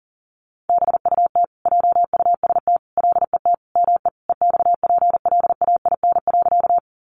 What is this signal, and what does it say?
Morse code, heard as a keyed tone, about 40 wpm, 700 Hz. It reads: BUT JUST LET ME EXPLAIN.
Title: Sentences from Top 1100 Words 40wpm